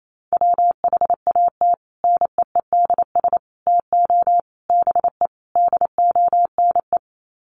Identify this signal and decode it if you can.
Morse code, heard as a keyed tone, about 28 wpm, 700 Hz. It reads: WHAT NEEDS TO BE DONE
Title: Sets of 5 Words Encore 28wpm